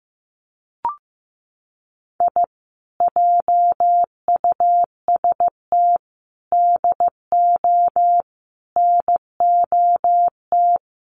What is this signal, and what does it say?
Morse code, heard as a keyed tone, about 15 wpm, 700 Hz. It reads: I JUST DO NOT
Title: Sets of 5 Words Encore2 15wpm